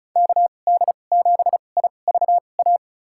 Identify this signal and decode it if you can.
Morse code, heard as a keyed tone, about 34 wpm, spaced as for 25 wpm, 700 Hz. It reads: KD7IVA
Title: USA Call Signs - 2x Character Spacing - 35wpm